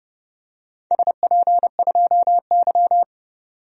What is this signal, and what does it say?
Morse code, heard as a keyed tone, about 31 wpm, 700 Hz. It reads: SP2Y
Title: DXCC-100 Call Signs - 4 Characters - 30wpm